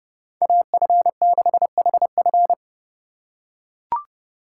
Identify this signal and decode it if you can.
Morse code, heard as a keyed tone, about 31 wpm, 700 Hz. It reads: AF6HF